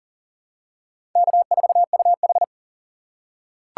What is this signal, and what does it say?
Morse code, heard as a keyed tone, about 40 wpm, 700 Hz. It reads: K4UH
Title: USA Call Signs - 4 Characters - 2x Character Spacing - 40wpm